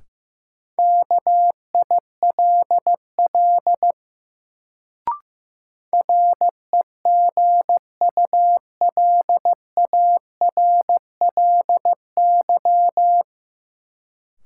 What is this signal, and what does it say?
Morse code, heard as a keyed tone, about 15 wpm, 700 Hz. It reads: KILL REGULARLY
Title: Sentences from Top 1400 Words - New Word Drill Alt - 15wpm